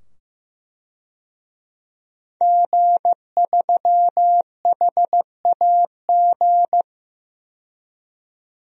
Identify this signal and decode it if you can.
Morse code, heard as a keyed tone, about 15 wpm, 700 Hz. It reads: G3HAG
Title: DXCC-100 Call Signs - 5 Characters - 15wpm